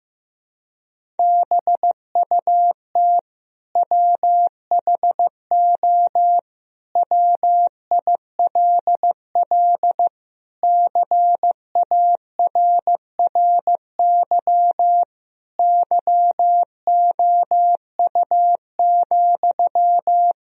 Morse code, heard as a keyed tone, about 15 wpm, 700 Hz. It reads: BUT WHO WILL CARRY YOU,
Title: Sentences from Top 1000 Words - Review Part 1 - 15wpm